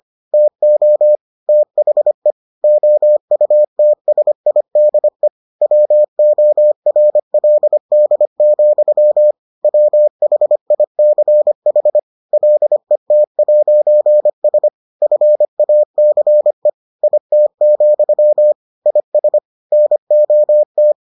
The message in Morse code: TO THE OUTSIDE WORLD, WHICH LET'S FACE IT, IS NOT

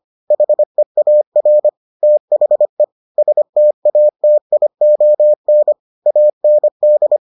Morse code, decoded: HEAR THE STATION AND

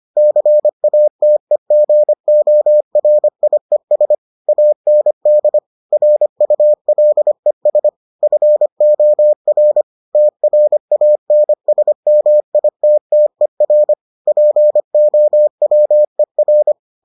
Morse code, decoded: CATEGORIES AND RULES FOR TRANSMITTER POWER

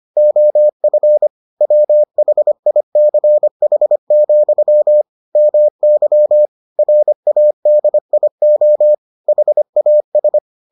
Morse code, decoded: OF WHICH, MY RADIO HAS

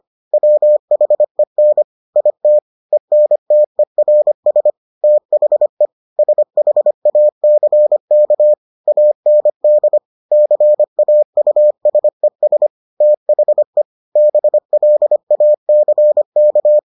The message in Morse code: WHEN IT ENTERS THE SHACK AND CAUSES THE BLACK

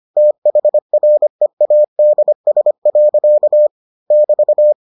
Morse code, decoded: THREADS. <BT>

